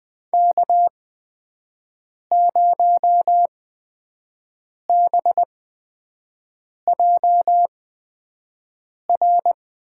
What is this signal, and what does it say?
Morse code, heard as a keyed tone, about 20 wpm, spaced as for 5 wpm, 700 Hz. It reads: K0BJR
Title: USA Call Signs - 5 Characters - 8x Character Spacing - 20wpm